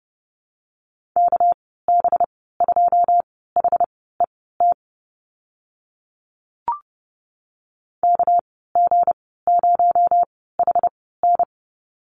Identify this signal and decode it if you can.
Morse code, heard as a keyed tone, about 30 wpm, spaced as for 17 wpm, 700 Hz. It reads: KB2HET KG0HN
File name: Lesson 37g - Letter x - Call Signs 30wpm